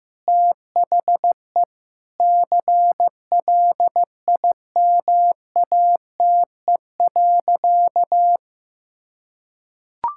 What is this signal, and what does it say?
Morse code, heard as a keyed tone, about 15 wpm, 700 Hz. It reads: THE CLIMATE.
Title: Sentences from Top 1600 Words 15wpm